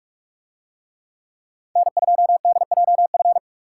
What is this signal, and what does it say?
Morse code, heard as a keyed tone, about 49 wpm, 700 Hz. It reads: N2DJF